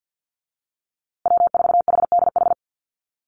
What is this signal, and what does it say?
Morse code, heard as a keyed tone, about 50 wpm, 700 Hz. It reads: W4HDH
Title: USA Call Signs Repeated 2x 50wpm